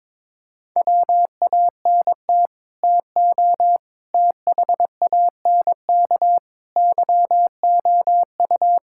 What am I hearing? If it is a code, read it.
Morse code, heard as a keyed tone, about 22 wpm, 700 Hz. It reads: WANT TO THANK YOU
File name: practice / 22wpm / Sets of 8 Words - 22wpm